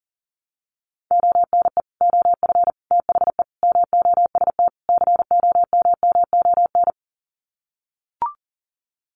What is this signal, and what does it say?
Morse code, heard as a keyed tone, about 37 wpm, 700 Hz. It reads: ONE OF THE MOST COMMON
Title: Sets of 5 Words Encore2 40wpm